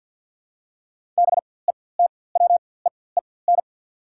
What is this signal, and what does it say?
Morse code, heard as a keyed tone, about 47 wpm, spaced as for 23 wpm, 690 Hz. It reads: BETWEEN
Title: 4x Character Spacing - Top 500 Words - 50wpm